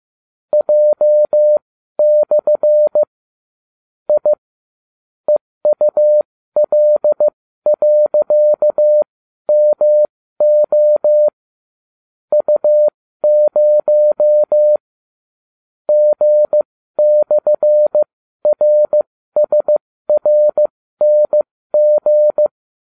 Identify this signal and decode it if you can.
Morse code, heard as a keyed tone, about 15 wpm, spaced as for 12 wpm, 600 Hz. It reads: J/ I EUL.MO U0 G/RSRNG